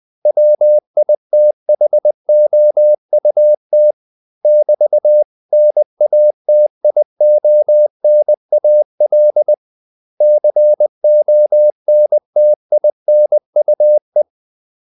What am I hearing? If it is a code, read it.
Morse code, heard as a keyed tone, about 20 wpm, 600 Hz. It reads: WITHOUT <BT>NATIONAL CONTINUE
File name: CW / Practice / 500MostCommonEnglishWords20